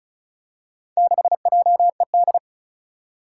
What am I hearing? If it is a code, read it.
Morse code, heard as a keyed tone, about 36 wpm, 700 Hz. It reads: 6JED
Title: Four Challenging Letter-Number Combinations 35wpm